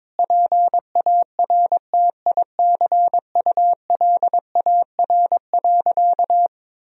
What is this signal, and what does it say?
Morse code, heard as a keyed tone, about 21 wpm, 700 Hz. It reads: PARTICULAR.
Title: Sentences from Top 1000 Words - Review Part 4 - 22wpm